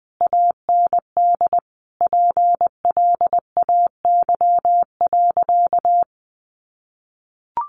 Morse code, heard as a keyed tone, about 20 wpm, 700 Hz. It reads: AND PLAY.
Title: Sentences from Top 1300 Words 20wpm